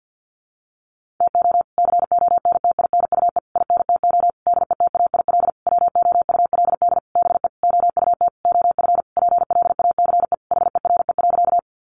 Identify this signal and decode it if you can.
Morse code, heard as a keyed tone, about 50 wpm, 700 Hz. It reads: TO CONTINUE INTO DETAIL WOULD BE OUT OF PLACE HERE.
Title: Sentences from Top 1500 Words 50wpm